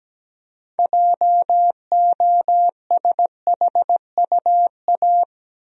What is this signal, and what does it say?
Morse code, heard as a keyed tone, about 17 wpm, 700 Hz. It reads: JOSHUA
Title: USA Common Names - 17wpm